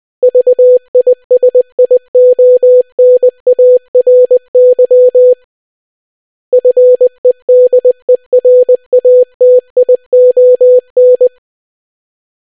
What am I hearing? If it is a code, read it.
Morse code, heard as a keyed tone, about 20 wpm, 500 Hz. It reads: VISIONARY FEDERATION